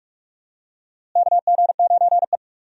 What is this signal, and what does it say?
Morse code, heard as a keyed tone, about 46 wpm, 700 Hz. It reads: KG9E
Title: USA Call Signs - 4 Characters - Repeated 2x - 45wpm